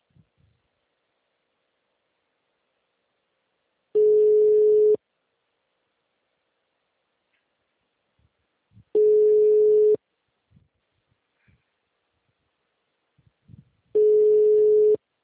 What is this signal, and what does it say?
Ringback tone (Europe, 425 Hz).
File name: out- -2016- - - .23